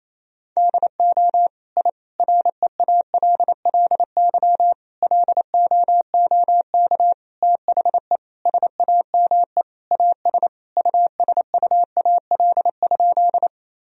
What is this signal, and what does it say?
Morse code, heard as a keyed tone, about 27 wpm, 700 Hz. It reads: DO I REALLY LOOK THE SAME AS USUAL?
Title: Sentences from Top 1000 Words - Review Part 2 - 28wpm